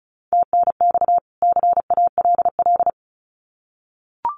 Morse code, heard as a keyed tone, about 34 wpm, 700 Hz. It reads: TNX CALL
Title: Common QSO Phrases 35wpm